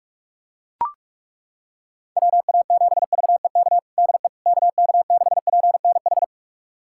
Morse code, heard as a keyed tone, about 41 wpm, 690 Hz. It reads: WA7VEK DE KK6PNH
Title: USA Callsign DE Callsign - 6 Characters - 45wpm